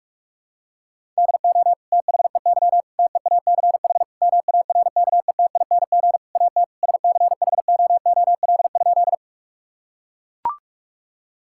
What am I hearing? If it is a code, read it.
Morse code, heard as a keyed tone, about 42 wpm, 700 Hz. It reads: DO THEY TEACH MARKETING AT SCHOOL?
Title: Sentences from Top 1000 Words - Review Part 5 - 45wpm